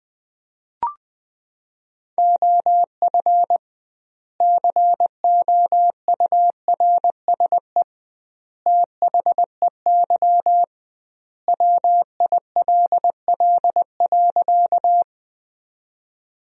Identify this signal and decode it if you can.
Morse code, heard as a keyed tone, about 20 wpm, 700 Hz. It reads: OF COURSE THEY WILL.
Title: 2x Word-Spacing - Sentences from Top 500 Words - 20wpm